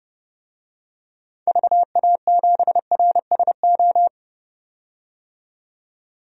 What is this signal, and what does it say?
Morse code, heard as a keyed tone, about 30 wpm, 700 Hz. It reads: VA7RSO
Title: DXCC-100 Call Signs - 6 Characters - 30wpm